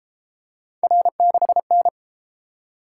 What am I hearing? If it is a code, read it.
Morse code, heard as a keyed tone, about 31 wpm, 700 Hz. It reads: R6N